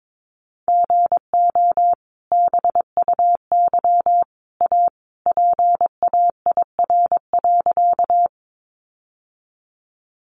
Morse code, heard as a keyed tone, about 22 wpm, 700 Hz. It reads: GO BUY A PAIR.
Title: Sentences from Top 1200 Words 22wpm